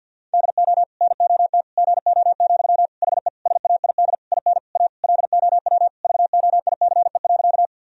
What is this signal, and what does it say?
Morse code, heard as a keyed tone, about 51 wpm, 700 Hz. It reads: DO NOT GO, HE SAID IN A LOW VOICE.
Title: Sentences from Top 1000 Words - Review Part 1 - 50wpm